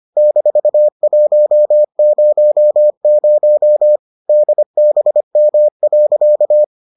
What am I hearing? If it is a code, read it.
Morse code, heard as a keyed tone, about 25 wpm, 600 Hz. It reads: -100 DBM.